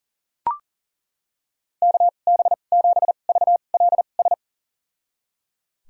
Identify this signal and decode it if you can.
Morse code, heard as a keyed tone, about 38 wpm, spaced as for 28 wpm, 700 Hz. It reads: KB7VLS
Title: USA Call Signs - 6 Characters - 2x Character Spacing - 40wpm